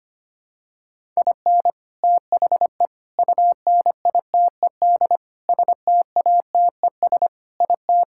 Morse code, heard as a keyed tone, about 25 wpm, 700 Hz. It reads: IN THE UNITED STATES IT
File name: practice / 25wpm / Sentences with 3-Word Phrases - 25wpm